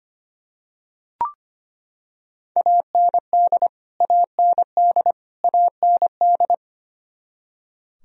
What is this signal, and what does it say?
Morse code, heard as a keyed tone, about 25 wpm, 700 Hz. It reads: AND AND AND
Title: Top 100 Words Repeated 25wpm